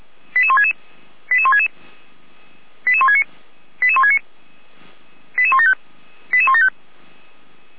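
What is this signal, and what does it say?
Five-tone selective call (ZVEI), heard as quick runs of five tones, tones of about 70 ms. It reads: 88166 88166 88169 88169 88175 88175
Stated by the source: ZVEI five-tone call received by SDR, fire brigade Orlamünde (feuerwehr-orlamuende).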